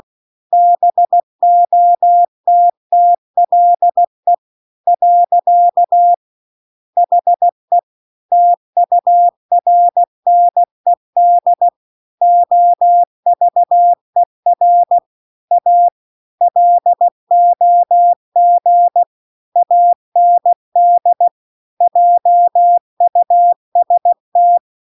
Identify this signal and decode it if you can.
Morse code, heard as a keyed tone, about 16 wpm, 700 Hz. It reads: BOTTLE . HE TURNED OVER A LOG AND JUST